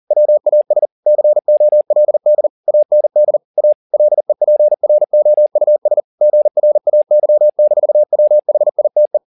Morse code, heard as a keyed tone, about 40 wpm, 600 Hz. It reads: WAS COLD AND A LEPROUS GRAY-WHITE
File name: the old man and the sea 2 40wpm